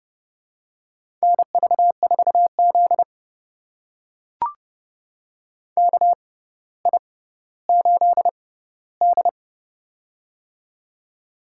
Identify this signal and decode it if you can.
Morse code, heard as a keyed tone, about 30 wpm, spaced as for 15 wpm, 700 Hz. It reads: NV4Z KI8D
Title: USA Call Signs - 6x Character Spacing - 30wpm